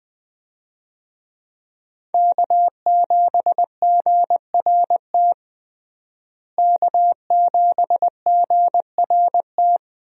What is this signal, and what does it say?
Morse code, heard as a keyed tone, about 20 wpm, 700 Hz. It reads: K7GRT K7GRT
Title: USA Call Signs - 5 Characters - Repeated 3x - 20wpm